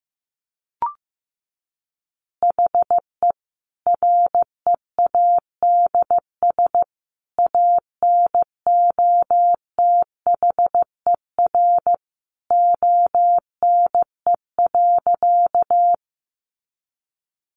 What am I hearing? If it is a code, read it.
Morse code, heard as a keyed tone, about 15 wpm, 700 Hz. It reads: HE READS ANOTHER ONE.
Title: Additional Easy Phrases 15wpm